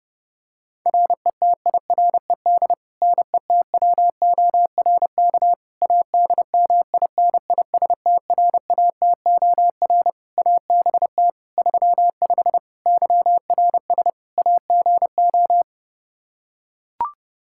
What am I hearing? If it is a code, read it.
Morse code, heard as a keyed tone, about 30 wpm, 700 Hz. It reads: RETIRED NETWORK ADMINISTRATOR ABT 35 YRS AGO
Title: Practice QSO - Exchange 3 - More Info - 30wpm